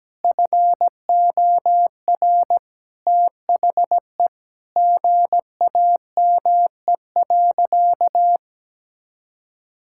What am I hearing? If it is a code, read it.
Morse code, heard as a keyed tone, about 17 wpm, 700 Hz. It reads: FOR THE GAME.
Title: Sentences from Top 1000 Words - Review Part 5 - 17wpm